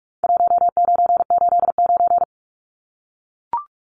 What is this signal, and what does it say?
Morse code, heard as a keyed tone, about 44 wpm, 700 Hz. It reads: 1989
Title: Numbers - 4-Digit 45wpm